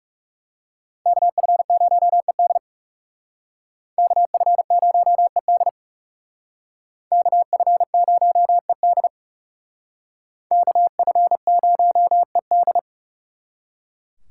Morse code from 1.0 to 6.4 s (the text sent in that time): KF0ED KF0ED